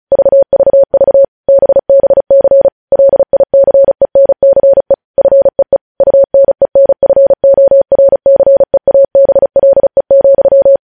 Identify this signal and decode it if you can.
Morse code, heard as a keyed tone, about 36 wpm, 550 Hz. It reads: VVV BBC LICENCE FEE UNENFORCEABLE,
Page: News headlines in Morse code at 35 wpm